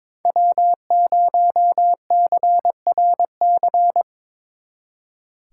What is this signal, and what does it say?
Morse code, heard as a keyed tone, about 25 wpm, 700 Hz. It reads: W0CRC
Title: Speed Racing - USA 1x3 Call Signs 22wpm